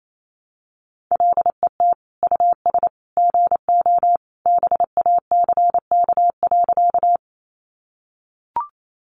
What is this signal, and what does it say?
Morse code, heard as a keyed tone, about 28 wpm, 700 Hz. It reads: LET US GO BACK.
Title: Sentences from Top 300 Words 28wpm